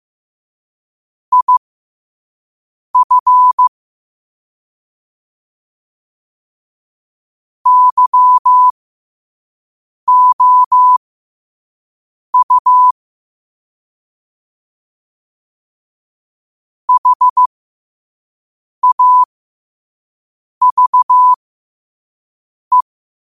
Morse code, heard as a keyed tone, about 15 wpm, spaced as for 5 wpm, 1000 Hz. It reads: IF YOU HAVE
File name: qotdfiles / 5wpm